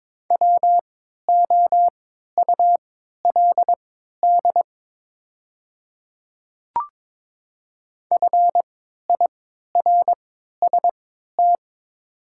Morse code, heard as a keyed tone, about 23 wpm, spaced as for 13 wpm, 700 Hz. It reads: WOULD FIRST
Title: 3x Character Spacing - Top 200 Words - 22wpm